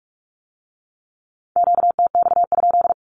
Morse code, heard as a keyed tone, about 46 wpm, 700 Hz. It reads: QTX?
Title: Q-Codes with Spoken Meaning - 45wpm